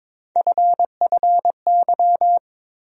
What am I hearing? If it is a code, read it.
Morse code, heard as a keyed tone, about 21 wpm, 700 Hz. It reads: FFY
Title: USA Call Sign Postfixes - 3 Letters 22wpm